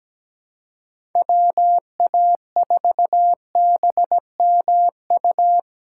Morse code, heard as a keyed tone, about 17 wpm, 700 Hz. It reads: WA4BMU
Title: USA Callsign DE Callsign 17wpm